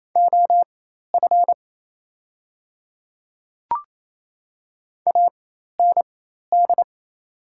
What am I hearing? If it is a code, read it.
Morse code, heard as a keyed tone, about 28 wpm, spaced as for 13 wpm, 700 Hz. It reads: OF AND
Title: 4x Character Spacing - Top 100 Words - 28wpm